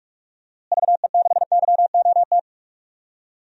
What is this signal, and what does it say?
Morse code, heard as a keyed tone, about 45 wpm, 700 Hz. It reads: VE6YOT
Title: DXCC-100 Call Signs - 6 Characters - Repeated 3x - 45wpm